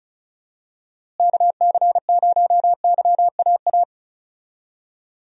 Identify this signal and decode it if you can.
Morse code, heard as a keyed tone, about 34 wpm, 700 Hz. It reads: KC0YAA
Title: USA Call Signs - 6 Characters - Repeated 3x - 35wpm